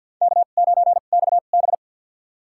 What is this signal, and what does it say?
Morse code, heard as a keyed tone, about 50 wpm, spaced as for 36 wpm, 690 Hz. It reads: K9XB
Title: USA Call Signs - 2x Character Spacing - 50wpm